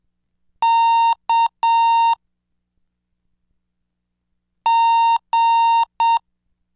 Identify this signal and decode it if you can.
Morse code, heard as a keyed tone, about 7 wpm, 920 Hz. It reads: K G
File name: cursos / CDestudo / 13 Track 13